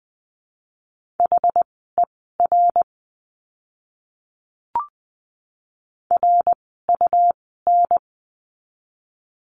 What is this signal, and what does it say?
Morse code, heard as a keyed tone, about 20 wpm, spaced as for 14 wpm, 700 Hz. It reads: HER RUN
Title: 115 Lesson 19f - Letter w - Words 20wpm